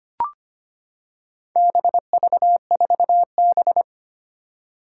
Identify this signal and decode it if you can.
Morse code, heard as a keyed tone, about 25 wpm, 700 Hz. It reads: BV4B
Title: Four Challenging Letter-Number Combinations 25wpm